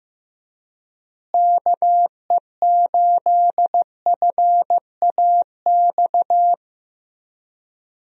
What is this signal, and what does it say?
Morse code, heard as a keyed tone, about 15 wpm, 700 Hz. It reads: KE8FAX
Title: USA Call Signs - 6 Characters - Repeated 3x - 15wpm